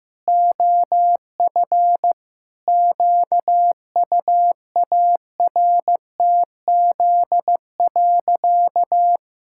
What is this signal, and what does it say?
Morse code, heard as a keyed tone, about 15 wpm, 700 Hz. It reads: OF QUARTZ.
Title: Pangram Challenge v2 - 15wpm